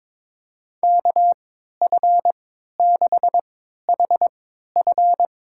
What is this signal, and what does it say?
Morse code, heard as a keyed tone, about 21 wpm, spaced as for 12 wpm, 700 Hz. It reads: KF6HF
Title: USA Call Signs - 3x Character Spacing - 22wpm